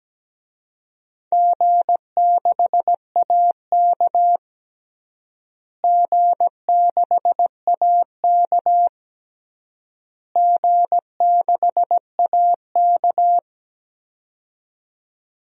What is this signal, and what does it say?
Morse code, heard as a keyed tone, about 17 wpm, 700 Hz. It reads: G6AK G6AK G6AK
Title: DXCC-100 Call Signs - 4 Characters - Repeated 3x - 17wpm